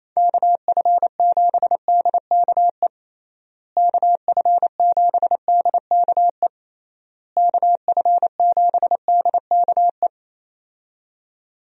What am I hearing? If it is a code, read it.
Morse code, heard as a keyed tone, about 27 wpm, 700 Hz. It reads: KF7DKE KF7DKE KF7DKE
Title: USA Call Signs - 6 Characters - Repeated 3x - 28wpm